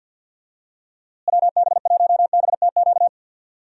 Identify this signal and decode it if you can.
Morse code, heard as a keyed tone, about 46 wpm, 700 Hz. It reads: WB1BTQ
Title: USA Call Signs - 6 Characters - Repeated 3x - 50wpm